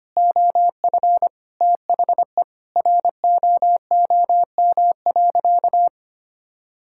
Morse code, heard as a keyed tone, about 25 wpm, 690 Hz. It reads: OF THE ROOM.
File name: practice / 25wpm / Sentences with 2-Word Phrases - 25wpm